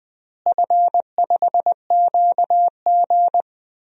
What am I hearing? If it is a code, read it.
Morse code, heard as a keyed tone, about 20 wpm, 700 Hz. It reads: F5QG
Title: DXCC-100 Call Signs - 4 Characters - Repeated 2x - 20wpm